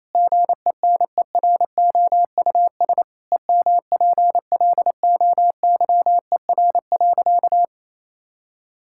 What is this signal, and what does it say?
Morse code, heard as a keyed tone, about 28 wpm, 700 Hz. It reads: GENEROUS EMPLOYER.
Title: Pangram Challenge v2 - 28wpm